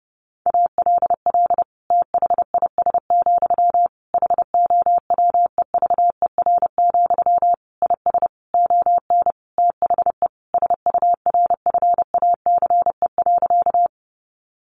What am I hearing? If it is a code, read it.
Morse code, heard as a keyed tone, about 30 wpm, 700 Hz. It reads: ALL THIS, HOWEVER, IS ON THE SURFACE.
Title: Sentences from Top 600 Words 30wpm